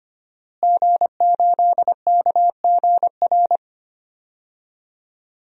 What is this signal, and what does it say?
Morse code, heard as a keyed tone, about 25 wpm, 700 Hz. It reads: G8KGR
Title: DXCC-100 Call Signs - 5 Characters - Repeated 2x - 25wpm